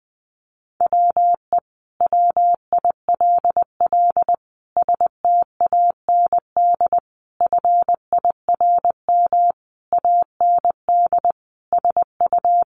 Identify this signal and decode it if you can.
Morse code, heard as a keyed tone, about 20 wpm, 700 Hz. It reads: WE WILL STAND FIRM AND SU
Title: Sentences from Top 1400 Words 20wpm